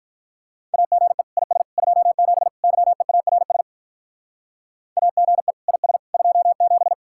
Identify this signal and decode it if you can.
Morse code, heard as a keyed tone, about 45 wpm, 700 Hz. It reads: AGE IS 27 YEARS AGE IS 27